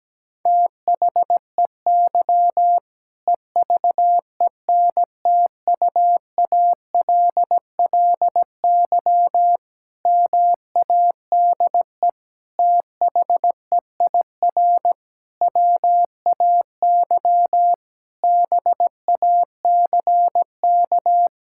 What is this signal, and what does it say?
Morse code, heard as a keyed tone, about 17 wpm, 700 Hz. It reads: THEY EVENTUALLY MADE THEIR WAY BACK
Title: Sentences from Top 1000 Words - Review Part 6 - 17wpm